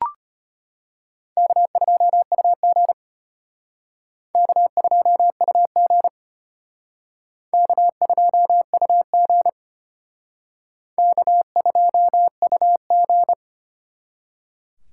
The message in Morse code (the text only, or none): K2UG K2UG K2UG K2UG